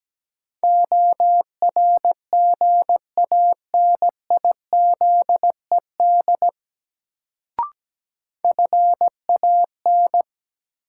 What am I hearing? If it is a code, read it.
Morse code, heard as a keyed tone, about 17 wpm, 700 Hz. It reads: ORGANIZED FAN
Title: Sentences from Top 1300 Words - New Word Drill Alt - 17wpm